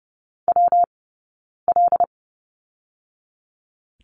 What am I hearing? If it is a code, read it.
Morse code, heard as a keyed tone, about 30 wpm, 700 Hz. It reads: W L
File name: Lesson 20d - Number 3 - Two Characters 30wpm